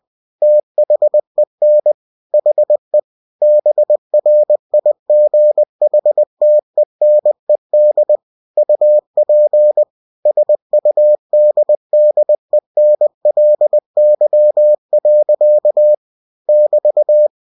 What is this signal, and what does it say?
Morse code, heard as a keyed tone, about 20 wpm, 600 Hz. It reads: THEN HE BRIGHTENED UP SUDDENLY. <BT>
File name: cw-books / winnie-the-pooh / winnie-20wpm-page-0025